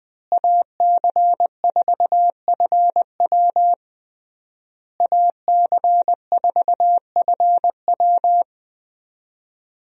Morse code, heard as a keyed tone, about 20 wpm, 700 Hz. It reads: AC4FW AC4FW
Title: USA Call Signs - 5 Characters - Repeated 3x - 20wpm